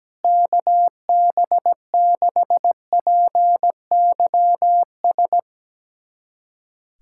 Morse code, heard as a keyed tone, about 17 wpm, 700 Hz. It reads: KB6PYS
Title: USA Callsign DE Callsign - 6 Characters - 17wpm